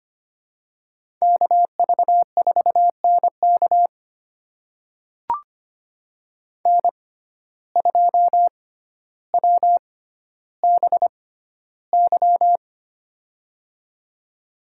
Morse code, heard as a keyed tone, about 25 wpm, spaced as for 13 wpm, 700 Hz. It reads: KV4NK N2WBY
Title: USA Call Signs - 5 Characters - 6x Character Spacing - 25wpm